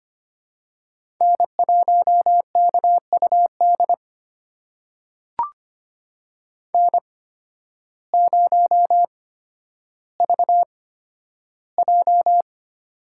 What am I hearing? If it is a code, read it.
Morse code, heard as a keyed tone, about 25 wpm, spaced as for 12 wpm, 700 Hz. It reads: N1KUD N0VJ
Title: USA Call Signs - 8x Character Spacing - 25wpm